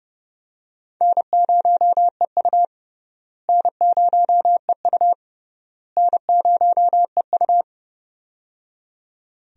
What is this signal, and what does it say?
Morse code, heard as a keyed tone, about 30 wpm, 700 Hz. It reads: N0EU N0EU N0EU
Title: USA Call Signs - 4 Characters - Repeated 3x - 30wpm